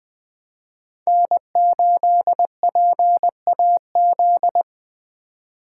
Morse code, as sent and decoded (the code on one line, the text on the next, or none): -. ---.. .--. .- --..
N8PAZ